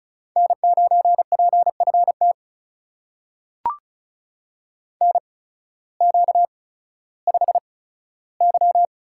Morse code, heard as a keyed tone, about 36 wpm, spaced as for 16 wpm, 700 Hz. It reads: N9PFT NQ5Y
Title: USA Call Signs - 8x Character Spacing - 35wpm